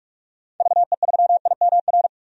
Morse code, heard as a keyed tone, about 43 wpm, 700 Hz. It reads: VE3IMR
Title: DXCC-100 Call Signs - 6 Characters - 45wpm